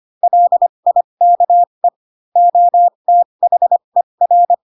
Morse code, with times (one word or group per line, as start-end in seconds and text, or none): 0.2-1.9 LIKE
2.4-4.5 OTHER